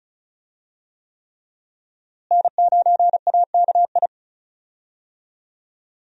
Morse code, heard as a keyed tone, about 36 wpm, 700 Hz. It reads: N9AKI
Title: USA Call Signs - 5 Characters - Repeated 3x - 35wpm